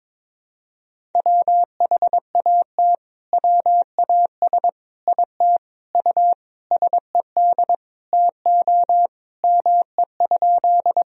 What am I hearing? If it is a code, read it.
Morse code, heard as a keyed tone, about 22 wpm, 700 Hz. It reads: WHAT WAS IT U SED TO ME?
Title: Sentences from Top 200 Words and Abbreviations 22wpm